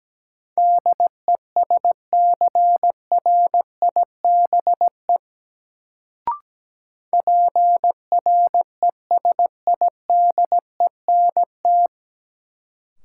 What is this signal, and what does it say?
Morse code, heard as a keyed tone, about 17 wpm, 700 Hz. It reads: DESCRIBE PRESIDENT
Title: Sentences from Top 1000 Words - New Word Drill - 17wpm